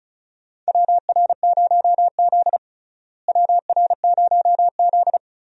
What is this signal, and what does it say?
Morse code, heard as a keyed tone, about 36 wpm, 700 Hz. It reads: WR0Z WR0Z
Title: USA Call Signs - 4 Characters - Repeated 2x - 35wpm